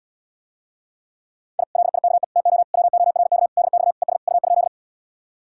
Morse code, heard as a keyed tone, about 77 wpm, 700 Hz. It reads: I LIKE TO WORK DX ES U?